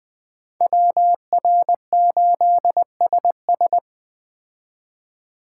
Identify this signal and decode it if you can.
Morse code, heard as a keyed tone, about 20 wpm, 700 Hz. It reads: WR8SS